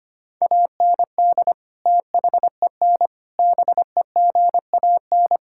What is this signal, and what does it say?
Morse code, heard as a keyed tone, about 25 wpm, 700 Hz. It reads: AND THEN BEGAN
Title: Sentences from Top 1000 Words - Review Part 5 - 25wpm